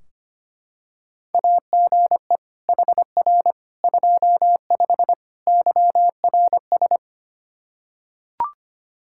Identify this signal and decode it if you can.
Morse code, heard as a keyed tone, about 25 wpm, 700 Hz. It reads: AGE HR 25 YRS
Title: K1USN SST QSO Practice 25wpm